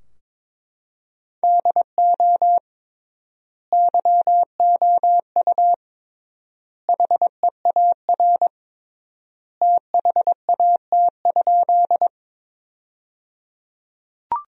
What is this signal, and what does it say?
Morse code, heard as a keyed tone, about 22 wpm, 700 Hz. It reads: DO YOU HEAR THAT?
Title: 3x Word-Spacing - Sentences from Top 300 Words - 22wpm